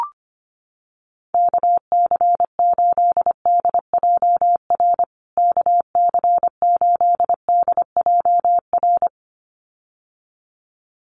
Morse code, heard as a keyed tone, about 25 wpm, 700 Hz. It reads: KC8DJR KC8DJR
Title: USA Call Signs Repeated 2x 25wpm